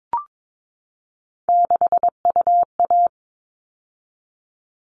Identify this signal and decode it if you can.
Morse code, heard as a keyed tone, about 22 wpm, 700 Hz. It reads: 6UA
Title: All Cut Numbers - 3-Digit 22wpm